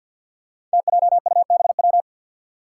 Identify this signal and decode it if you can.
Morse code, heard as a keyed tone, about 51 wpm, 700 Hz. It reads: TJUBW
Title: Five Challenging Letter-Number Combinations 50wpm